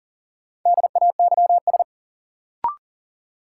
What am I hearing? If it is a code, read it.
Morse code, heard as a keyed tone, about 40 wpm, 700 Hz. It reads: DAYS
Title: Sets of 5 Words Encore - New Word Drill - 40wpm